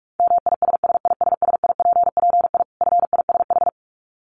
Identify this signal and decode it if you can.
Morse code, heard as a keyed tone, about 44 wpm, 700 Hz. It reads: MISSISSIPPI FISH
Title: Dit Challenge 45wpm